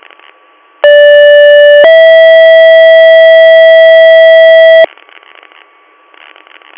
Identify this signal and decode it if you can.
Two-tone page, heard as a one-second tone, then a three-second tone, about 580 then 650 Hz, with noise. It is heard from 0.8 to 4.8 s.